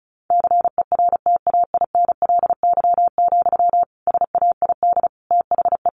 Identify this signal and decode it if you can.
Morse code, heard as a keyed tone, about 34 wpm, 700 Hz. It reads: CERTAINLY, SAID THE
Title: Sentences from Top 1500 Words 35wpm